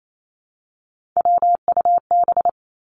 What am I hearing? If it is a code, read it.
Morse code, heard as a keyed tone, about 28 wpm, 700 Hz. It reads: WUB
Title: Three Challenging Letter-Number Combinations 28wpm